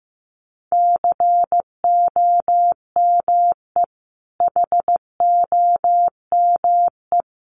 Morse code, heard as a keyed tone, about 15 wpm, 700 Hz. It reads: COME HOME
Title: Sentences from Top 800 Words 15wpm